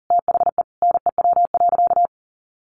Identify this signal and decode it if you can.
Morse code, heard as a keyed tone, about 40 wpm, 700 Hz. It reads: THE NEW.
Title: Sentences from Top 1300 Words 40wpm